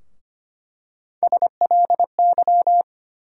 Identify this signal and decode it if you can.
Morse code, heard as a keyed tone, about 25 wpm, 700 Hz. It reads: SLY
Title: USA Call Sign Postfixes - 3 Letters 25wpm